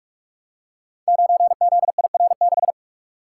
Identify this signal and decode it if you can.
Morse code, heard as a keyed tone, about 44 wpm, 700 Hz. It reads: 9ZIR6
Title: Code Groups - 5 Letter:Numbers - 45wpm